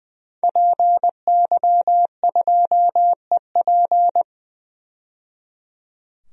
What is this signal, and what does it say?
Morse code, heard as a keyed tone, about 20 wpm, 700 Hz. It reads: PY2EP
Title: DXCC-100 Call Signs - 5 Characters - 20wpm